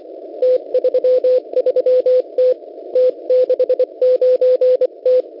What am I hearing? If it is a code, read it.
Morse code, heard as a keyed tone, about 24 wpm, 510 Hz. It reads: T33T T69T
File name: Unid- -0700utc-4735khz